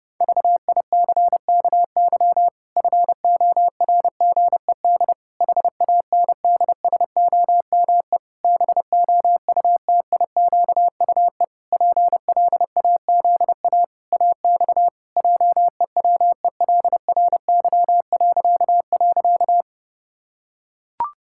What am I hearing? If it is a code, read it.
Morse code, heard as a keyed tone, about 30 wpm, 700 Hz. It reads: VICKY FORGED HANDSOME BOUTIQUE PLAZA AX JEWELRY..
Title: Pangram Challenge v2 - 30wpm